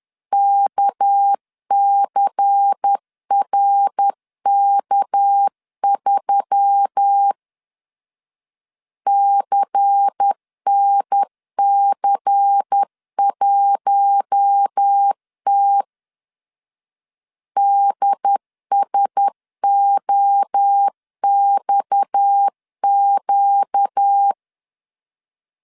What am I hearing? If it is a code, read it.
Morse code, heard as a keyed tone, about 10 wpm, 800 Hz. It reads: KCRK3 CNC1T DSOXQ
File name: CW / DET / 1 6 1500C S A 10wpm 4141